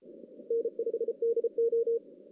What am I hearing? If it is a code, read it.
Morse code, heard as a keyed tone, about 34 wpm, 460 Hz. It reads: N5DO
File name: 2016 JAN CW NAQP / 2016 Jan naqp k5ot - 0-11